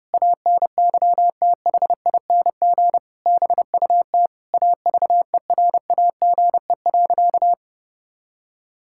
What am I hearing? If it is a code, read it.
Morse code, heard as a keyed tone, about 30 wpm, 700 Hz. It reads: ANYTHING BUT AVERAGE.
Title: Sentences from Top 1000 Words - Review Part 6 - 30wpm